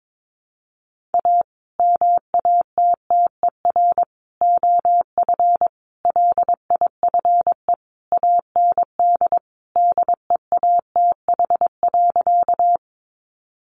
Morse code, heard as a keyed tone, about 22 wpm, 700 Hz. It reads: A MATTER OF LIFE AND DEATH.
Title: Sentences from Top 1100 Words 22wpm